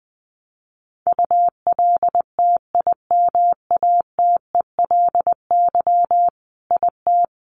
Morse code, heard as a keyed tone, about 20 wpm, 700 Hz. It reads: ULTIMATELY IT
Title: Sentences from Top 1200 Words 20wpm